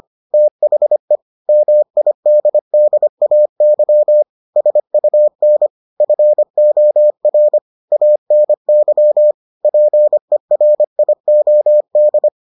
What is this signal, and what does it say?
Morse code, heard as a keyed tone, about 25 wpm, 600 Hz. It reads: THE MIDDAY SUN FOR ANY PERIOD